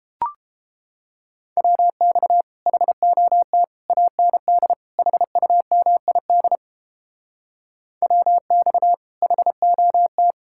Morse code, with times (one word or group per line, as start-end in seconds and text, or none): 1.6-2.4 WX
2.7-3.6 HOT
3.9-4.7 AND
5.0-6.6 HUMID
8.0-8.9 WX
9.2-10.3 HOT